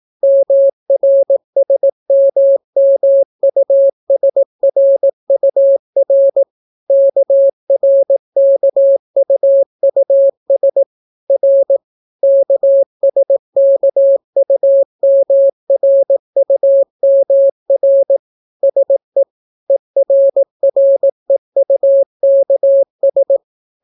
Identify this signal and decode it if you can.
Morse code, heard as a keyed tone, about 18 wpm, 550 Hz. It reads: MRSMMUSRUR KRKUUS R KSKUMRUMR SE ERREUKS